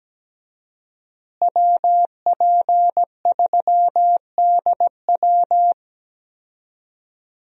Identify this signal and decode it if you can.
Morse code, heard as a keyed tone, about 17 wpm, 700 Hz. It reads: WP3DW
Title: USA Call Signs - 5 Characters - Repeated 2x - 17wpm